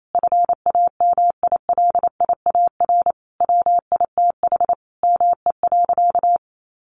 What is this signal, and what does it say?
Morse code, heard as a keyed tone, about 28 wpm, 700 Hz. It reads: FAMILIAR WITH ME.
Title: Sentences from Top 1000 Words 28wpm